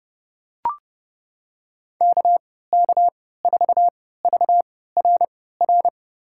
Morse code, heard as a keyed tone, about 32 wpm, spaced as for 17 wpm, 700 Hz. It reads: KK4VRR
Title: USA Call Signs - 6 Characters - 3x Character Spacing - 30wpm